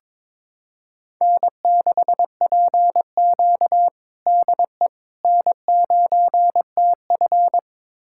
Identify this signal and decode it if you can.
Morse code, heard as a keyed tone, about 22 wpm, 700 Hz. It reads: N6PQ DE N9TF
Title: USA Callsign DE Callsign - 4 Characters - 22wpm